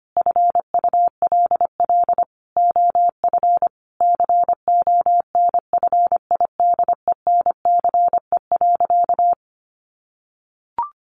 Morse code, heard as a keyed tone, about 25 wpm, 700 Hz. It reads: FULL OF CONFIDENCE.
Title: Sentences from Top 1500 Words 25wpm